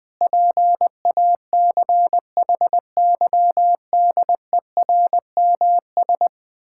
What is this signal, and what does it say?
Morse code, heard as a keyed tone, about 20 wpm, 700 Hz. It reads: PACHYDERMS
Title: Pangram Challenge v2 - 20wpm